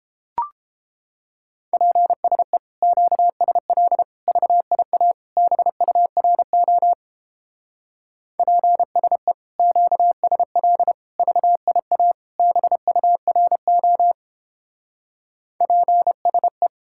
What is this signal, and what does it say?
Morse code, heard as a keyed tone, about 31 wpm, 700 Hz. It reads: PSE QSL VIA BURO PSE QSL VIA BURO PSE